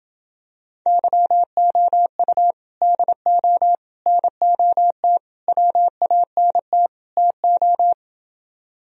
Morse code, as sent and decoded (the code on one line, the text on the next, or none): -.-- --- ..- / -.. --- / -. --- - / .-- .- -. - / - ---
YOU DO NOT WANT TO